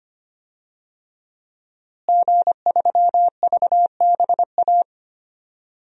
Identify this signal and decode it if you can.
Morse code, heard as a keyed tone, about 25 wpm, 700 Hz. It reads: G3VBA
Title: DXCC-100 Call Signs - 5 Characters - Repeated 3x - 25wpm